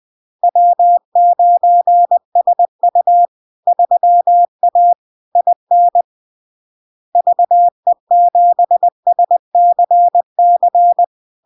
Morse code, heard as a keyed tone, about 20 wpm, 700 Hz. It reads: W9SU 3A IN VE7SCC
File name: LARK-FD-2022-Log-in-CW-20wpm-700-Hz